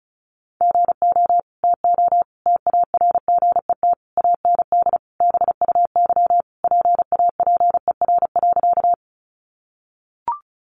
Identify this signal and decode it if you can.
Morse code, heard as a keyed tone, about 34 wpm, 700 Hz. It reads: GO TO TARGET AND BUY PAPER.
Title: Sentences from Top 1200 Words 35wpm